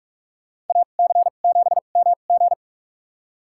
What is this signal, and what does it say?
Morse code, heard as a keyed tone, about 47 wpm, spaced as for 33 wpm, 700 Hz. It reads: AC7MG
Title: USA Call Signs - 2x Character Spacing - 45wpm